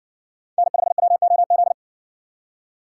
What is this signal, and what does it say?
Morse code, heard as a keyed tone, about 60 wpm, 700 Hz. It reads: N5WOZ